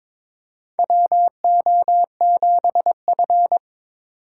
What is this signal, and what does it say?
Morse code, heard as a keyed tone, about 22 wpm, 700 Hz. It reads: WO7F
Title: USA Call Signs - 4 Characters - 4x Character Spacing - 22wpm